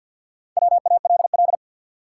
Morse code, heard as a keyed tone, about 52 wpm, 700 Hz. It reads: WALL
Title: Sentences from Top 1000 Words - Review Part 6 - New Word Drill - 50wpm